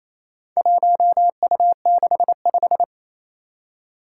Morse code, heard as a keyed tone, about 27 wpm, 700 Hz. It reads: 1U65
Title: Four Challenging Letter-Number Combinations 28wpm